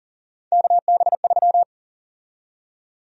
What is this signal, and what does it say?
Morse code, heard as a keyed tone, about 37 wpm, 700 Hz. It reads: KB3
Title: USA Call Sign Prefixes - 40wpm